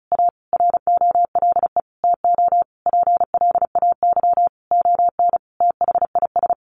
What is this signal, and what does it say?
Morse code, heard as a keyed tone, about 34 wpm, 700 Hz. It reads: A ROLE TO PLAY ON THIS
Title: Sentences from Top 1500 Words 35wpm